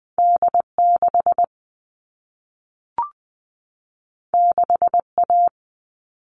Morse code, heard as a keyed tone, about 20 wpm, 700 Hz. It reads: D 6 6 A
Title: All Cut Numbers - 2-Digit 20wpm